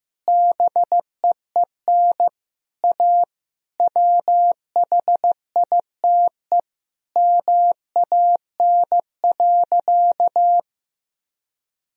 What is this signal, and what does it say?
Morse code, heard as a keyed tone, about 15 wpm, 700 Hz. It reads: BEEN A WHITE MAN.
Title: Sentences from Top 1000 Words - Review Part 1 - 15wpm